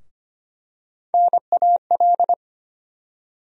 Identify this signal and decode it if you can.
Morse code, heard as a keyed tone, about 25 wpm, 700 Hz. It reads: NAL
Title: USA Call Sign Postfixes - 3 Letters 25wpm